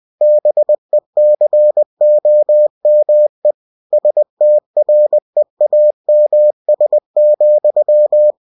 Morse code, heard as a keyed tone, about 20 wpm, 600 Hz. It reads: BECOME STREAMS,
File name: cw-books / winnie-the-pooh / winnie-20wpm-page-0091